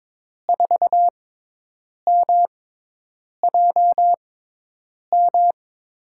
Morse code, heard as a keyed tone, about 22 wpm, spaced as for 8 wpm, 700 Hz. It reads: 4MJM
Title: USA Call Signs - 6x Character Spacing - 22wpm